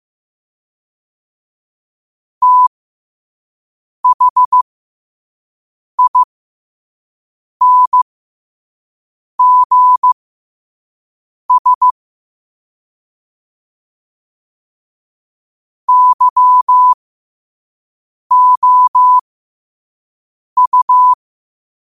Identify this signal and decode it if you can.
Morse code, heard as a keyed tone, about 15 wpm, spaced as for 5 wpm, 1000 Hz. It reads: THINGS YOU
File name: qotdfiles / 5wpm